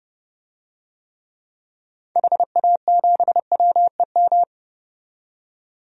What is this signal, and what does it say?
Morse code, heard as a keyed tone, about 29 wpm, 690 Hz. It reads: HA7WEM
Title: DXCC-100 Call Signs - 6 Characters - 30wpm